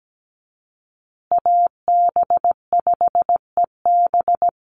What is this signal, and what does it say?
Morse code, heard as a keyed tone, about 17 wpm, 700 Hz. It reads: AB5EB
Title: Call Signs Repeated 2x - 17wpm